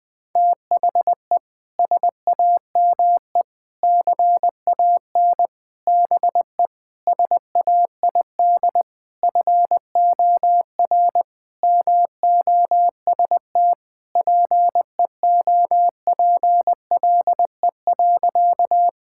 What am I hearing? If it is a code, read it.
Morse code, heard as a keyed tone, about 19 wpm, 700 Hz. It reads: THE SAME CAN BE SAID FOR MOST PEOPLE.
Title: Sentences with 6-Word Phrases - Spoken Beginning - 20wpm